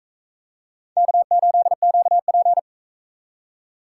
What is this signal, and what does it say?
Morse code, heard as a keyed tone, about 40 wpm, 700 Hz. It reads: K8QP